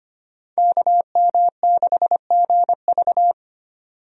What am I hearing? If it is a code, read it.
Morse code, heard as a keyed tone, about 25 wpm, 700 Hz. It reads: KM6GV
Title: USA Call Signs - 5 Characters - 25wpm